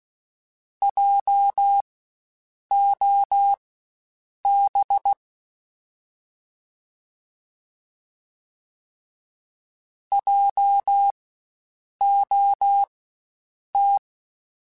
Morse code, heard as a keyed tone, about 16 wpm, spaced as for 7 wpm, 800 Hz. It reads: JOB JOT
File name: Spaced Alphas / Spaced Alphas mp3s / Js sp @15